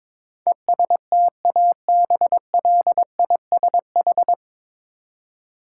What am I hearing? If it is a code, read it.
Morse code, heard as a keyed tone, about 22 wpm, 700 Hz. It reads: ESTABLISH
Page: Morse Code Ninja practice 22wpm